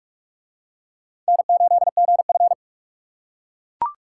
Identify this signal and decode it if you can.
Morse code, heard as a keyed tone, about 44 wpm, 700 Hz. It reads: N8GF